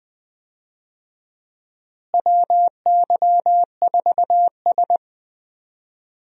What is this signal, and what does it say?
Morse code, heard as a keyed tone, about 20 wpm, 700 Hz. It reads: WY4S
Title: USA Call Signs - 4 Characters - Repeated 2x - 20wpm